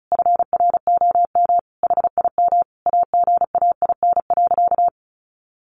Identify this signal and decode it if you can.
Morse code, heard as a keyed tone, about 34 wpm, 700 Hz. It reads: FROM HIM AGAIN.
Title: Sentences from Top 1300 Words 35wpm